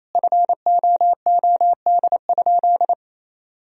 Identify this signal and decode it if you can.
Morse code, heard as a keyed tone, about 28 wpm, 700 Hz. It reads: FOOD?
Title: Sentences from Top 1000 Words - Review Part 2 - 28wpm